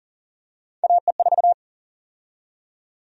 Morse code, heard as a keyed tone, about 40 wpm, 700 Hz. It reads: AE4
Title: USA Call Sign Prefixes - 40wpm